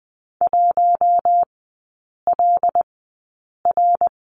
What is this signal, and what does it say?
Morse code, heard as a keyed tone, about 20 wpm, spaced as for 8 wpm, 700 Hz. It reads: 1LR
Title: 107 Lesson 18e - Letter m - Three Characters 20wpm